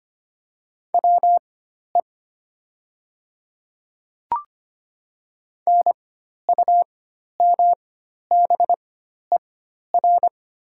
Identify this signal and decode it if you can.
Morse code, heard as a keyed tone, about 25 wpm, spaced as for 12 wpm, 700 Hz. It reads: WE NUMBER
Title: 4x Character Spacing - Top 200 Words - 25wpm